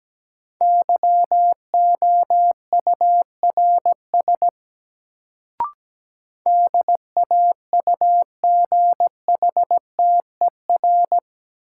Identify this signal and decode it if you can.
Morse code, heard as a keyed tone, about 17 wpm, 700 Hz. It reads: YOURS DAUGHTER
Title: Sentences from Top 1500 Words - New Word Drill - 17wpm